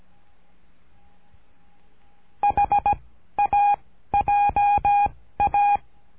Morse code, heard as a keyed tone, about 17 wpm, 820 Hz. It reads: H A J A
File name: cursos / CURSO DE TELEGRAFIA / Telegrafia - Trilha 13